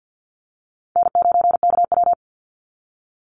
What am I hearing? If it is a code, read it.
Morse code, heard as a keyed tone, about 48 wpm, 700 Hz. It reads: N9KW
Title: NA SOTA Chaser Callsigns 50wpm